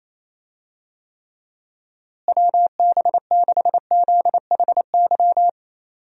Morse code, heard as a keyed tone, about 28 wpm, 700 Hz. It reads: WB6ZHY
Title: USA Call Signs - 6 Characters - Repeated 3x - 28wpm